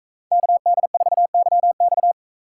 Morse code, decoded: KD4YX

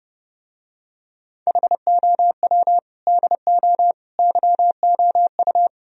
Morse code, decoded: HOW DO YOU

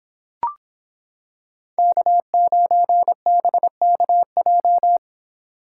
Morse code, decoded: K9BKJ

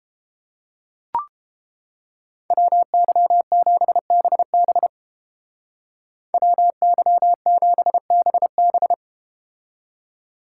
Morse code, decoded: WY7BB WY7BB

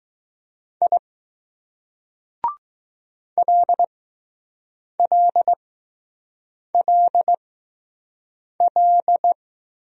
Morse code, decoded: I LLLL

